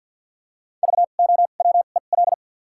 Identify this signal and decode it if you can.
Morse code, heard as a keyed tone, about 53 wpm, spaced as for 37 wpm, 700 Hz. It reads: VOWEL
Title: 2x Character Spacing - Top 500 Words - 50wpm